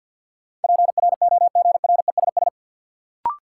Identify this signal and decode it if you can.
Morse code, heard as a keyed tone, about 53 wpm, 700 Hz. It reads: PROGRESS
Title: Sentences from Top 1400 Words - New Word Drill - 50wpm